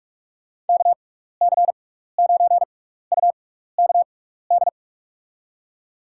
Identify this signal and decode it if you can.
Morse code, heard as a keyed tone, about 47 wpm, spaced as for 16 wpm, 700 Hz. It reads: KC9UKD
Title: USA Call Signs - 6 Characters - 6x Character Spacing - 45wpm